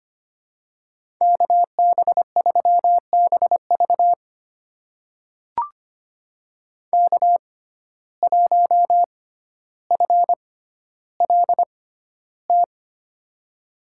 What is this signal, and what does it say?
Morse code, heard as a keyed tone, about 25 wpm, spaced as for 13 wpm, 700 Hz. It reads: KB3BV K1FLT
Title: USA Call Signs - 5 Characters - 6x Character Spacing - 25wpm